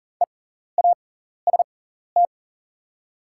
Morse code, heard as a keyed tone, about 40 wpm, spaced as for 14 wpm, 690 Hz. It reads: EAST
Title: 6x Character Spacing - Top 500 Words - 40wpm